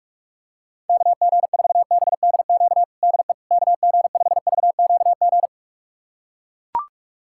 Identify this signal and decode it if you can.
Morse code, heard as a keyed tone, about 41 wpm, 700 Hz. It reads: KG4BDQ DE KG5VQG
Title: USA Callsign DE Callsign 45wpm